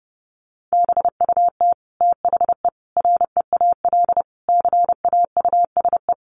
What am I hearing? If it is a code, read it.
Morse code, heard as a keyed tone, about 30 wpm, 700 Hz. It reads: BUT THE REAL CAUSE
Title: Sentences from Top 1200 Words 30wpm